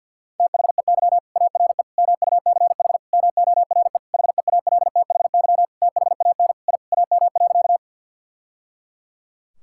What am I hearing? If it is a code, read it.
Morse code, heard as a keyed tone, about 46 wpm, 700 Hz. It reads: THEY ARE MUCH MORE HEALTHY THAN I AM.
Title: Sentences from Top 1000 Words - Review Part 5 - 50wpm